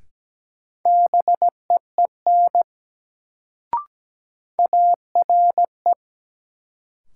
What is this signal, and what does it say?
Morse code, heard as a keyed tone, about 17 wpm, 700 Hz. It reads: BEEN ARE
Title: Sentences from Top 100 Words - New Word Drill - 17wpm